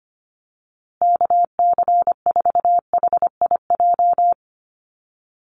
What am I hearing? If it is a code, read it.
Morse code, heard as a keyed tone, about 25 wpm, 700 Hz. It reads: KC4HIJ
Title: USA Call Signs Repeated 2x 25wpm